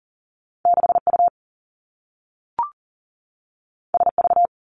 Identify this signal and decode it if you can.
Morse code, heard as a keyed tone, about 38 wpm, 700 Hz. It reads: 6U SV
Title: Two Challenging Letter-Number Combinations 40wpm